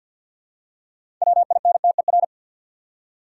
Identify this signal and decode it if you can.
Morse code, heard as a keyed tone, about 48 wpm, 700 Hz. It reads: WINTER
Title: Six Letter Words - 50wpm